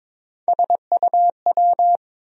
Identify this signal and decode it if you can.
Morse code, heard as a keyed tone, about 22 wpm, 700 Hz. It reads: SUW
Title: USA Call Sign Postfixes - 3 Letters 22wpm